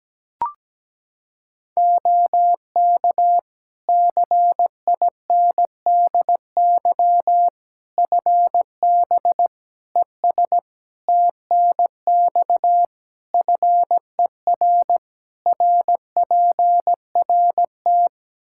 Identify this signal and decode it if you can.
Morse code, heard as a keyed tone, about 17 wpm, 700 Hz. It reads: OK CINDY FB ES TNX FER RPRT
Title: K1USN SST QSO Practice 17wpm